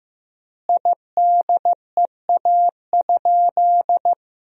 Morse code, heard as a keyed tone, about 16 wpm, 700 Hz. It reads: IDEA?
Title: 4x Word-Spacing - Sentences from Top 400 Words - 15wpm